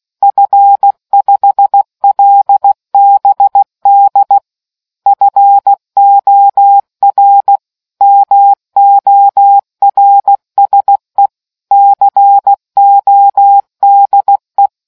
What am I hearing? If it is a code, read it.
Morse code, heard as a keyed tone, about 16 wpm, 800 Hz. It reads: F5LBD FOR MORSE CODE